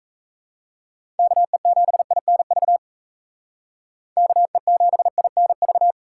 Morse code, heard as a keyed tone, about 34 wpm, 700 Hz. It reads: KE7INV KE7INV
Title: Speed Racing - USA 2x3 Call Signs 28wpm